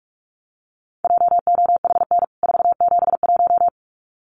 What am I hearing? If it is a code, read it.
Morse code, heard as a keyed tone, about 44 wpm, 700 Hz. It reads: JOHN 471
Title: CWT Exchange 45wpm